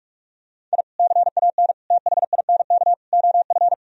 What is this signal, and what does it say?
Morse code, heard as a keyed tone, about 46 wpm, 700 Hz. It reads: I CAN THINK OF
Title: Sentences from Top 1000 Words - Review Part 1 - 45wpm